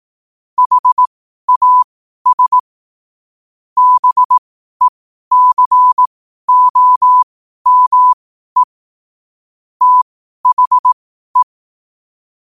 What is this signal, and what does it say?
Morse code, heard as a keyed tone, about 18 wpm, spaced as for 12 wpm, 1000 Hz. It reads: HAS BECOME THE
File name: qotdfiles / 12wpm